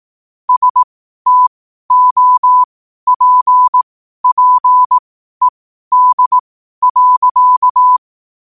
Morse code, heard as a keyed tone, about 18 wpm, spaced as for 12 wpm, 1000 Hz. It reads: STOPPED.
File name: qotdfiles / 12wpm